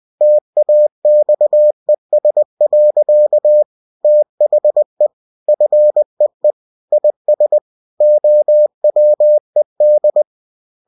Morse code, heard as a keyed tone, about 20 wpm, 600 Hz. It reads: TAXES. THE FEE IS OWED
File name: cw-books / winnie-the-pooh / winnie-20wpm-page-0117